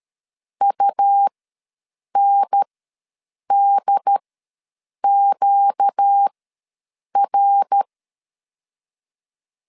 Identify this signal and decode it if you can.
Morse code, heard as a keyed tone, about 13 wpm, spaced as for 7 wpm, 800 Hz. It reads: UNDQR
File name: LEZIONI / 5 LBVC / 5 L ws 12wpm 2958